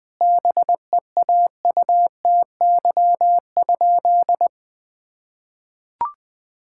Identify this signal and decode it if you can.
Morse code, heard as a keyed tone, about 20 wpm, 700 Hz. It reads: BEAUTY?
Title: Sentences from Top 1000 Words - Review Part 5 - 20wpm